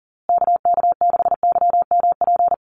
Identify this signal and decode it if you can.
Morse code, heard as a keyed tone, about 39 wpm, 700 Hz. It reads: KK6YMP
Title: USA Call Signs Repeated 3x 40wpm